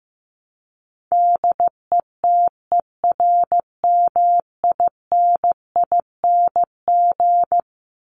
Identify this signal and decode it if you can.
Morse code, heard as a keyed tone, about 15 wpm, 700 Hz. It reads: DETERMINING
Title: ING Suffix - Top 100 Words with Base Words 15wpm